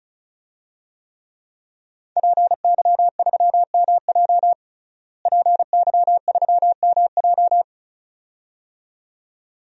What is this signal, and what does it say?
Morse code, heard as a keyed tone, about 36 wpm, 700 Hz. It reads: PY3MJ PY3MJ
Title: DXCC-100 Call Signs - 5 Characters - Repeated 2x - 35wpm